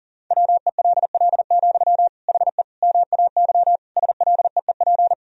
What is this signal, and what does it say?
Morse code, heard as a keyed tone, about 40 wpm, 700 Hz. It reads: WELL, HE MAY SLEEP
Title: Sentences from Top 1000 Words - Review Part 4 - 40wpm